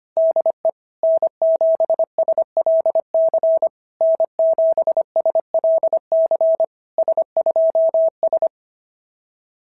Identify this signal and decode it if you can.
Morse code, heard as a keyed tone, about 24 wpm, 650 Hz. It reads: DE N7SLC N7SLC S2S
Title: SOTA QSO Practice 25wpm